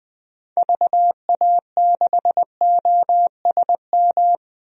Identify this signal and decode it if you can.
Morse code, heard as a keyed tone, about 20 wpm, 700 Hz. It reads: VA6OSM
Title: DXCC-100 Call Signs - 6 Characters - Repeated 2x - 20wpm